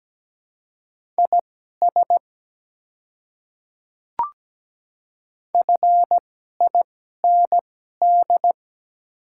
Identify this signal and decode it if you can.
Morse code, heard as a keyed tone, about 17 wpm, spaced as for 12 wpm, 700 Hz. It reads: IS FIND
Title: 2x Character Spacing - Top 100 Words - 17wpm